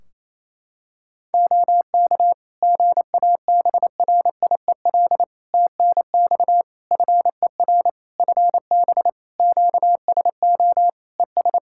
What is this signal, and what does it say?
Morse code, heard as a keyed tone, about 28 wpm, 700 Hz. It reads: OK GABRIEL TNX FER FB QSO ES
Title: Practice QSO - Exchange 4 - Ending - 28wpm